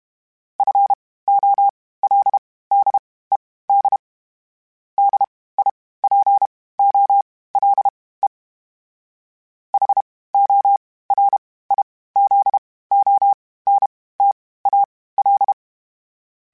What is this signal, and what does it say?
Morse code, heard as a keyed tone, about 32 wpm, spaced as for 18 wpm, 800 Hz. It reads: FOLDED DIPOLE HORIZONTAL
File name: (spaced) / FromSMs spaced mp3s / Antennas sp@30